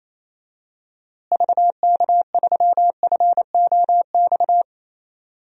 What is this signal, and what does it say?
Morse code, heard as a keyed tone, about 28 wpm, 700 Hz. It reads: VK3FOX
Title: DXCC-100 Call Signs - 6 Characters - Repeated 3x - 28wpm